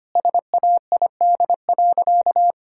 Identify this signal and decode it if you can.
Morse code, heard as a keyed tone, about 25 wpm, 700 Hz. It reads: SAID.
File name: practice / 25wpm / Sentences with 2-Word Phrases - 25wpm